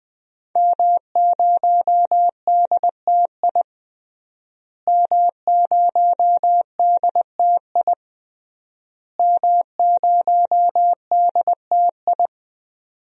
Morse code, heard as a keyed tone, about 20 wpm, 700 Hz. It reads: M0DTI M0DTI M0DTI
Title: DXCC-100 Call Signs - 5 Characters - Repeated 3x - 20wpm